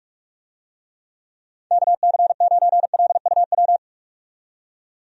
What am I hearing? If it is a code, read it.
Morse code, heard as a keyed tone, about 46 wpm, 700 Hz. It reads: KC9LUW
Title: USA Call Signs - 6 Characters - 45wpm